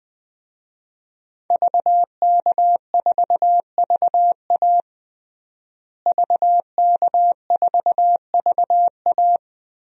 Morse code, heard as a keyed tone, about 20 wpm, 700 Hz. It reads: VK4VA VK4VA
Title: DXCC-100 Call Signs - 5 Characters - Repeated 3x - 20wpm